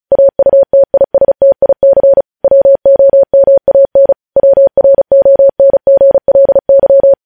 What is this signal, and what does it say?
Morse code, heard as a keyed tone, about 36 wpm, 550 Hz. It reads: AUTISTIC WOMAN WRONGLY
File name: podcast / News headlines-1x-35x35